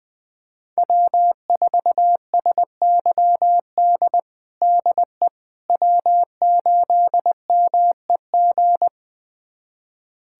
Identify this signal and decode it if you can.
Morse code, heard as a keyed tone, about 21 wpm, 700 Hz. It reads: W4SYD DE W8MEG
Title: USA Callsign DE Callsign - 5 Characters - 20wpm